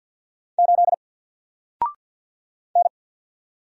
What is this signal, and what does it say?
Morse code, heard as a keyed tone, about 50 wpm, 700 Hz. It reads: 8 N
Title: Single Letter-Number 50wpm